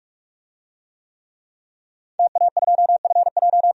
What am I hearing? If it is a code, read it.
Morse code, heard as a keyed tone, about 42 wpm, 700 Hz. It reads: TA2FJ
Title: DXCC-100 Call Signs - 5 Characters - 45wpm